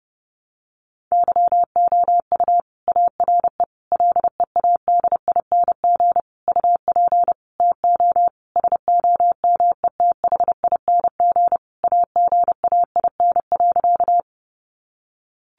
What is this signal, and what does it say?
Morse code, heard as a keyed tone, about 30 wpm, 700 Hz. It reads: YOU ARE LEADING UP TO SOMETHING AGAIN.
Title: Sentences from Top 1100 Words 30wpm